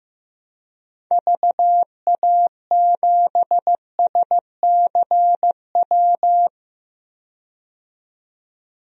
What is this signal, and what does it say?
Morse code, heard as a keyed tone, about 15 wpm, 700 Hz. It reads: VA7SCW
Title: DXCC-100 Call Signs - 6 Characters - 15wpm